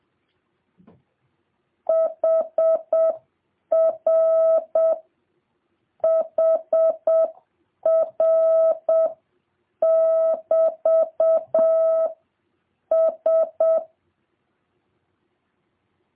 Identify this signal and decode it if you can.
Morse code, heard as a keyed tone, about 7 wpm, 1300 Hz, with noise. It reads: HR HR <BT> S